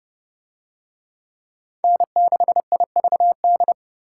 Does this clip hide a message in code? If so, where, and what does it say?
Yes: Morse code from 1.8 to 3.7 s, heard as a keyed tone, about 30 wpm, 700 Hz. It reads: N6IVD